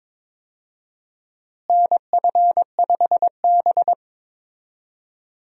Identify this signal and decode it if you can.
Morse code, heard as a keyed tone, about 22 wpm, 700 Hz. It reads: NF5B
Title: USA Call Signs - 4 Characters - Repeated 2x - 22wpm